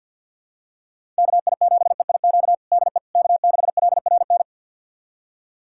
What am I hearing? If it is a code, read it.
Morse code, heard as a keyed tone, about 51 wpm, 700 Hz. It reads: KI7EIQ DE K6LRN
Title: USA Callsign DE Callsign 50wpm